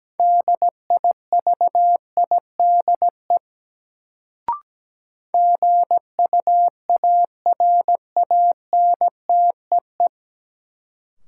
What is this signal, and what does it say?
Morse code, heard as a keyed tone, about 17 wpm, 700 Hz. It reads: DIVIDE GUARANTEE
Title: Sentences from Top 1600 Words - New Word Drill - 17wpm